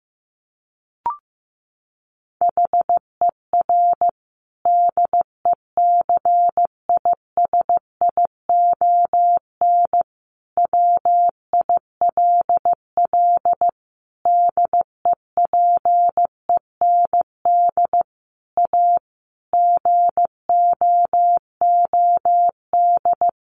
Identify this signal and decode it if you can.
Morse code, heard as a keyed tone, about 15 wpm, 700 Hz. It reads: HER DECISION WILL DEPEND A GOOD
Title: Sentences from Top 1300 Words 15wpm